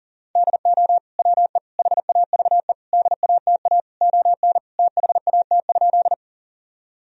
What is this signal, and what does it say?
Morse code, heard as a keyed tone, about 40 wpm, 700 Hz. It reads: DO WE HAVE DATA ON THAT?
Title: Sentences from Top 1000 Words - Review Part 3 - 40wpm